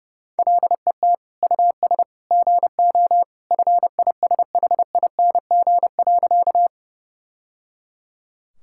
Morse code, heard as a keyed tone, about 32 wpm, 700 Hz. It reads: LET US GO FISHING.
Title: Sentences from Top 1000 Words - Review Part 5 - 30wpm